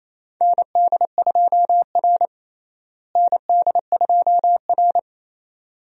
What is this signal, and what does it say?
Morse code, heard as a keyed tone, about 28 wpm, 700 Hz. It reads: ND2R ND2R
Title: USA Call Signs - 4 Characters - Repeated 2x - 28wpm